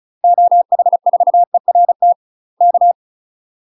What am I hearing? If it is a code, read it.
Morse code, heard as a keyed tone, about 34 wpm, 700 Hz. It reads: OH4ERT K
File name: UK MASTER QSOs TRAINING/35WPM